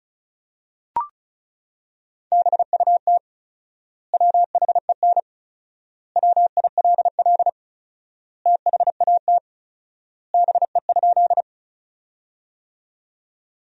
Morse code, heard as a keyed tone, about 36 wpm, 700 Hz. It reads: BUT WHEN WILL THAT BE?
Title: 4x Word-Spacing - Sentences from Top 100 Words - 35wpm